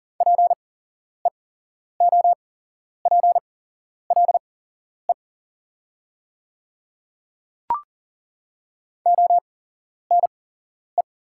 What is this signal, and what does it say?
Morse code, heard as a keyed tone, about 39 wpm, spaced as for 11 wpm, 700 Hz. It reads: PEOPLE ONE
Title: 8x Character Spacing - Top 100 Words - 40wpm